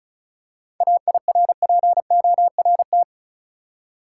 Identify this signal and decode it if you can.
Morse code, heard as a keyed tone, about 34 wpm, 700 Hz. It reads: AIRPORT
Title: Seven Letter Words - 35wpm